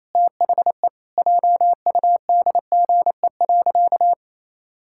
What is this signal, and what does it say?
Morse code, heard as a keyed tone, about 28 wpm, 700 Hz. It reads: THE JUDGE.
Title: Pangram Challenge v2 - 28wpm